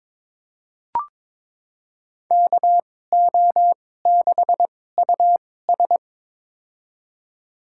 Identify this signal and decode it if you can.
Morse code, heard as a keyed tone, about 21 wpm, spaced as for 16 wpm, 700 Hz. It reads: KO6US
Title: USA Call Signs - 5 Characters - 2x Character Spacing - 22wpm